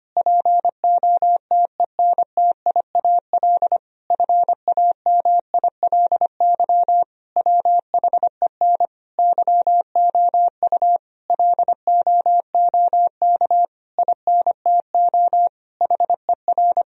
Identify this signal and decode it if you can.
Morse code, heard as a keyed tone, about 25 wpm, 700 Hz. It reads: POTENTIAL FAMILY WHEN YOU LOOK INTO HER
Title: Sentences from Top 1000 Words - Review Part 4 - 25wpm